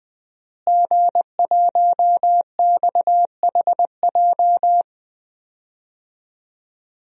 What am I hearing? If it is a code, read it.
Morse code, heard as a keyed tone, about 20 wpm, 700 Hz. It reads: G1XHJ
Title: DXCC-100 Call Signs - 5 Characters - Repeated 3x - 20wpm